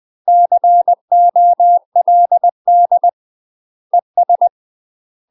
Morse code, heard as a keyed tone, about 20 wpm, 700 Hz. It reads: COLD ES
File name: UK MASTER QSOs TRAINING / 20WPM / UK MASTER QSOs 0042